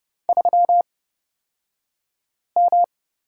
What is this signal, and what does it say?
Morse code, heard as a keyed tone, about 30 wpm, spaced as for 5 wpm, 700 Hz. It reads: 3M
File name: Lesson 37d - Letter x - Two Characters 30wpm